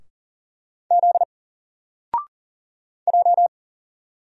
Morse code, heard as a keyed tone, about 38 wpm, 700 Hz. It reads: Z J
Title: Single Letter-Number 40wpm